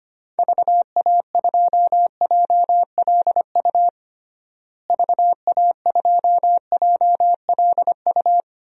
Morse code, heard as a keyed tone, about 25 wpm, 700 Hz. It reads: VA2JLU VA2JLU
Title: DXCC-100 Call Signs - 6 Characters - Repeated 3x - 25wpm